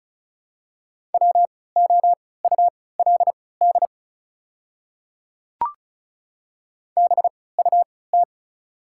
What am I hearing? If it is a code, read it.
Morse code, heard as a keyed tone, about 36 wpm, spaced as for 20 wpm, 700 Hz. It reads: WOULD BUT
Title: 3x Character Spacing - Top 100 Words - 35wpm